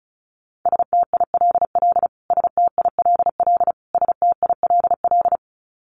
Morse code, heard as a keyed tone, about 34 wpm, 700 Hz. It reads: STILL STILL STILL
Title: Top 500 Words Repeated 35wpm